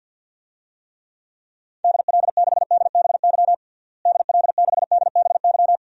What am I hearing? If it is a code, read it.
Morse code, heard as a keyed tone, about 48 wpm, 700 Hz. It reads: DL6DBY DL6DBY
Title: DXCC-100 Call Signs - 6 Characters - Repeated 3x - 50wpm